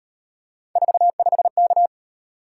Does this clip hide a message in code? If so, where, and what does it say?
Yes: Morse code from 0.8 to 1.9 s, heard as a keyed tone, about 35 wpm, 700 Hz. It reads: 45K